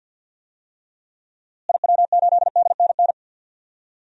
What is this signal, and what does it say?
Morse code, heard as a keyed tone, about 51 wpm, 700 Hz. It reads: IW8DNN
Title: DXCC-100 Call Signs - 6 Characters - Repeated 3x - 50wpm